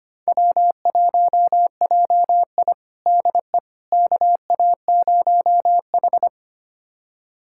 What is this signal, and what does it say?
Morse code, heard as a keyed tone, about 25 wpm, 700 Hz. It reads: W1JI DE KA0H
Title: USA Callsign DE Callsign - 4 Characters - 25wpm